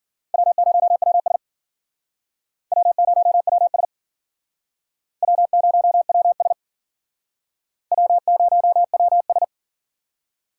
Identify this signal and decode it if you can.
Morse code, heard as a keyed tone, about 50 wpm, 700 Hz. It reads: W0WS W0WS W0WS W0WS